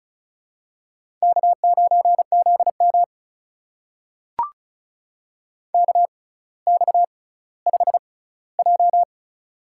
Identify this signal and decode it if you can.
Morse code, heard as a keyed tone, about 36 wpm, spaced as for 18 wpm, 700 Hz. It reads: K9ZM KX5J
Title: USA Call Signs - 4 Characters - 6x Character Spacing - 35wpm